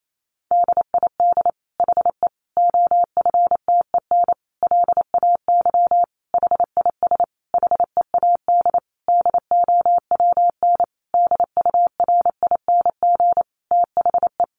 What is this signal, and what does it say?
Morse code, heard as a keyed tone, about 28 wpm, 700 Hz. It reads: DID HE OFTEN LAY HIS HEAD DOWN DURING THE
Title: Sentences from Top 1000 Words 28wpm